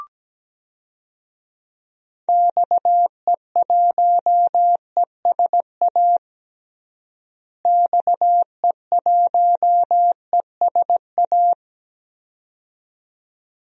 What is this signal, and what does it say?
Morse code, heard as a keyed tone, about 17 wpm, 700 Hz. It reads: XE1ESA XE1ESA
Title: DXCC-100 Call Signs - 6 Characters - Repeated 2x - 17wpm